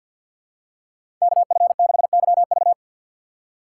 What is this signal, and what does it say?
Morse code, heard as a keyed tone, about 52 wpm, 700 Hz. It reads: KF6YV
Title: USA Call Signs - 5 Characters - 8x Character Spacing - 50wpm